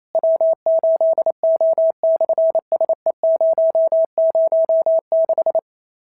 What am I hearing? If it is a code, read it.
Morse code, heard as a keyed tone, about 28 wpm, 650 Hz. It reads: W8O/SE006
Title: SOTA QSO Practice 28wpm